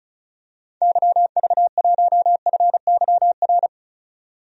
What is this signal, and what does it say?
Morse code, heard as a keyed tone, about 36 wpm, 700 Hz. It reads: YV1FYR
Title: DXCC-100 Call Signs - 6 Characters - 35wpm